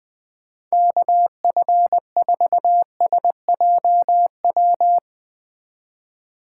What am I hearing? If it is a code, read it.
Morse code, heard as a keyed tone, about 20 wpm, 700 Hz. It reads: KF4SJW
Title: Speed Racing - USA Call Signs 17wpm